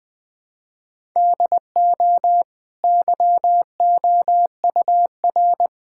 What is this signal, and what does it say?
Morse code, heard as a keyed tone, about 20 wpm, 700 Hz. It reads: DO YOUR
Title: Easy Sentences with Hints - 20wpm